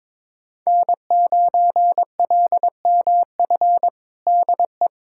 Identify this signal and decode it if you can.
Morse code, heard as a keyed tone, about 22 wpm, 700 Hz. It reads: N9LMF DE
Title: USA Callsign DE Callsign - 5 Characters - 22wpm